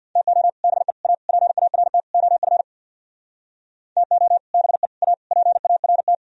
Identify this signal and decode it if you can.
Morse code, heard as a keyed tone, about 64 wpm, 700 Hz. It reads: TO BE A PART OF TO BE A PART